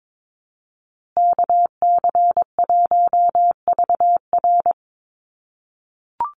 Morse code, heard as a keyed tone, about 22 wpm, 700 Hz. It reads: KC1VR
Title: USA Call Signs Repeated 3x 22wpm